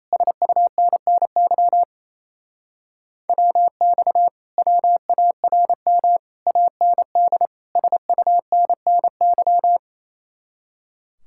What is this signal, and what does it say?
Morse code, heard as a keyed tone, about 28 wpm, 700 Hz. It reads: SUNNY WX WARM AND SUNNY
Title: Speed-Racing Common QSO Phrases Encore - 28wpm